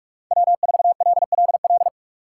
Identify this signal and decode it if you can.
Morse code, heard as a keyed tone, about 46 wpm, 700 Hz. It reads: W4LLL
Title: USA Call Signs - 5 Characters - 2x Character Spacing - 45wpm